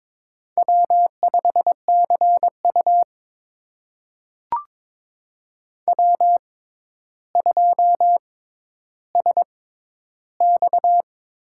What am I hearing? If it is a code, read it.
Morse code, heard as a keyed tone, about 21 wpm, spaced as for 11 wpm, 700 Hz. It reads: W5CU W2SX
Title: USA Call Signs - 4 Characters - 6x Character Spacing - 22wpm